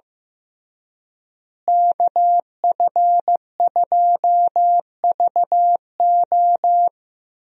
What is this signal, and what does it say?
Morse code, heard as a keyed tone, about 15 wpm, 700 Hz. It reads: KF2VO
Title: USA Call Signs - 5 Characters - Repeated 3x - 15wpm